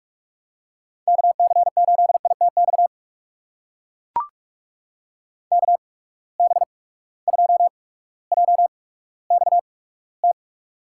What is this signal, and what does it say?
Morse code, heard as a keyed tone, about 41 wpm, spaced as for 19 wpm, 700 Hz. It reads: KC8ITX KB2JXT
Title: USA Call Signs - 6 Characters - 8x Character Spacing - 45wpm